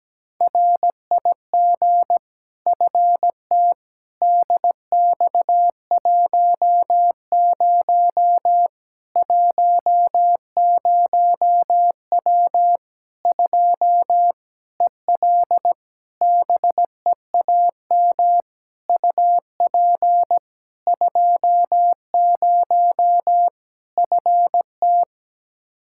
Morse code, heard as a keyed tone, about 17 wpm, 700 Hz. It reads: RIG FT DX10 10W 2 EL BEAM UP 20 FT
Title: Practice QSO - Exchange 2 - Info - 17wpm